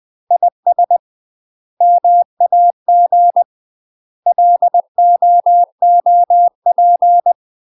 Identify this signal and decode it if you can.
Morse code, heard as a keyed tone, about 20 wpm, 700 Hz. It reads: IS MAG LOOP
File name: UK MASTER QSOs TRAINING / 20WPM / UK MASTER QSOs 0014